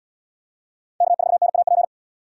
Morse code, heard as a keyed tone, about 78 wpm, 700 Hz. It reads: K4NAP